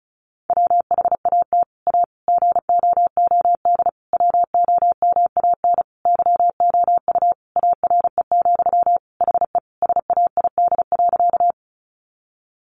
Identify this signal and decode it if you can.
Morse code, heard as a keyed tone, about 34 wpm, 700 Hz. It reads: WHAT A GOOD WOMAN YOU ARE, HE SAID.
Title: Sentences from Top 1100 Words 35wpm